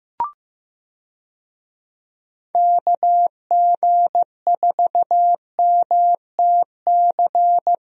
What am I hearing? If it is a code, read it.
Morse code, heard as a keyed tone, about 15 wpm, 700 Hz. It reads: KG4MTC
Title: USA Call Signs - 6 Characters - Repeated 3x - 15wpm